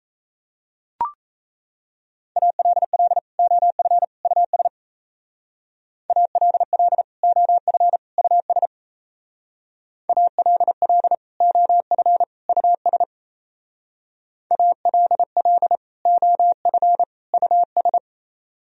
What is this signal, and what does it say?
Morse code, heard as a keyed tone, about 33 wpm, 700 Hz. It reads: ALL OF US ALL OF US ALL OF US ALL OF US
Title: Speed-Racing - Sets of 3 Words 28wpm